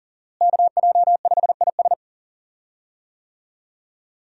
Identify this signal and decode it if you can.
Morse code, heard as a keyed tone, about 40 wpm, 700 Hz. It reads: KJ5IS
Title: USA Call Signs - 5 Characters - Repeated 2x - 40wpm